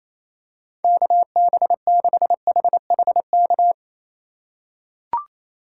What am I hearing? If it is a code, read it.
Morse code, heard as a keyed tone, about 28 wpm, 700 Hz. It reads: KB6HHK
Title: USA Call Signs - 6 Characters - 2x Character Spacing - 28wpm